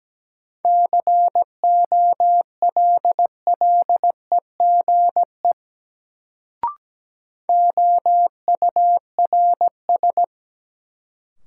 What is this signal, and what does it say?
Morse code, heard as a keyed tone, about 17 wpm, 700 Hz. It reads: COLLEGE OURS
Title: Sentences from Top 1300 Words - New Word Drill Alt - 17wpm